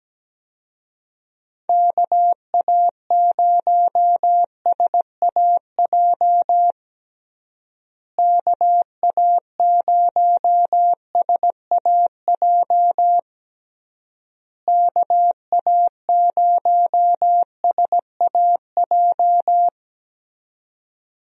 Morse code, heard as a keyed tone, about 17 wpm, 700 Hz. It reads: KA0SAJ KA0SAJ KA0SAJ
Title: USA Call Signs - 6 Characters - Repeated 3x - 17wpm